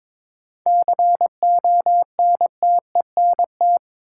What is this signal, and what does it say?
Morse code, heard as a keyed tone, about 22 wpm, 700 Hz. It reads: CONTENT
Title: Sentences from Top 900 Words - New Word Drill - 22wpm